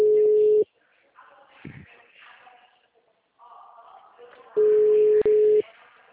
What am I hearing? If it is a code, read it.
Ringback tone (Europe, 425 Hz), with speech.